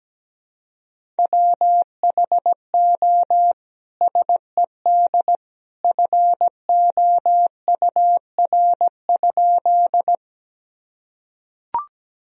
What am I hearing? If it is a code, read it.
Morse code, heard as a keyed tone, about 17 wpm, 700 Hz. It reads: WHO SED FOUR?
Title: Sentences from Top 300 Words and Abbreviations 17wpm